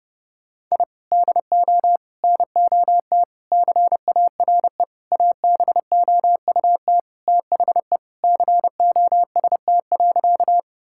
Morse code, heard as a keyed tone, about 32 wpm, 700 Hz. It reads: I DO NOT CARE ABOUT THE COST.
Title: Sentences from Top 1000 Words - Review Part 5 - 30wpm